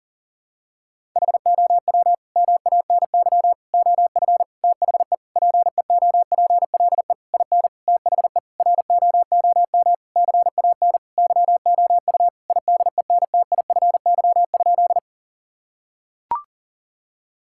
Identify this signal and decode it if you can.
Morse code, heard as a keyed tone, about 40 wpm, 690 Hz. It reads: HOW MANY OF THE PEOPLE IN THE ROOM CAN YOU IDENTIFY?
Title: Sentences from Top 1000 Words - Review Part 6 - 40wpm